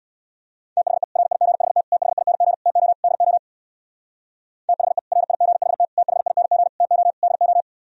Morse code, heard as a keyed tone, about 64 wpm, 700 Hz. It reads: THE RIGHT THING TO DO THE RIGHT THING TO DO